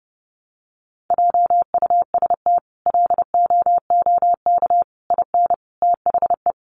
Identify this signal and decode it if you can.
Morse code, heard as a keyed tone, about 30 wpm, 700 Hz. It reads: JUST LOOK IN THE
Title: Sentences from Top 1300 Words 30wpm